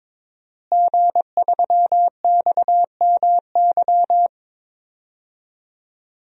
Morse code, heard as a keyed tone, about 22 wpm, 700 Hz. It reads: G3XMY
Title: DXCC-100 Call Signs - 5 Characters - Repeated 2x - 22wpm